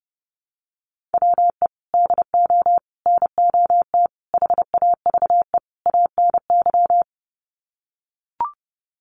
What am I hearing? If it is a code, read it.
Morse code, heard as a keyed tone, about 30 wpm, 700 Hz. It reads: WE DO NOT HAVE ANY
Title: Sets of 5 Words 30wpm